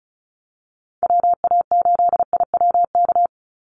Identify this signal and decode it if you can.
Morse code, heard as a keyed tone, about 34 wpm, 700 Hz. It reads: WA8IWK
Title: Call Signs - 35wpm